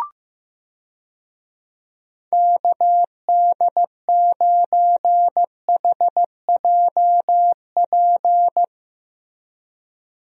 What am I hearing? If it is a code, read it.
Morse code, heard as a keyed tone, about 15 wpm, 700 Hz. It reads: KD9HJP
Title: USA Call Signs - 6 Characters - Repeated 2x - 15wpm